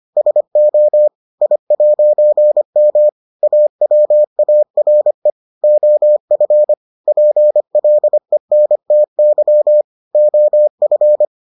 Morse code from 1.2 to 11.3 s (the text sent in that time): I'M AWARE OF PLENTY OF